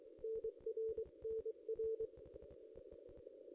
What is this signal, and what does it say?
Morse code, heard as a keyed tone, about 23 wpm, 440 Hz. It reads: NR NR
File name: contests / ss-cw-k5tr-op-n5rz -1900